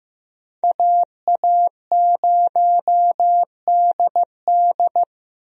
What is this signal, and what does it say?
Morse code, heard as a keyed tone, about 16 wpm, 700 Hz. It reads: AA0DD
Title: USA Call Signs - 5 Characters - Repeated 3x - 15wpm